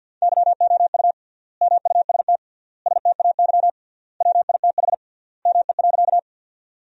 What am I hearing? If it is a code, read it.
Morse code, heard as a keyed tone, about 50 wpm, 700 Hz. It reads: YOU MUST STAY WITH ME.
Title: 3x Word-Spacing - Sentences from Top 500 Words - 50wpm